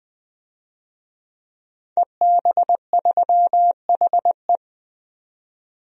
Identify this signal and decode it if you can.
Morse code, heard as a keyed tone, about 20 wpm, 700 Hz. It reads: EB3HE
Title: DXCC-100 Call Signs - 5 Characters - 20wpm